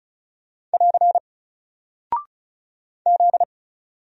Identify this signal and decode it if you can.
Morse code, heard as a keyed tone, about 36 wpm, 700 Hz. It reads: <AR> Z
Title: Single Character (Letter:Number:Punctuation:Prosign) Alt 35wpm